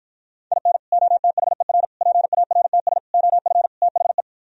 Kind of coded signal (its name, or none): Morse code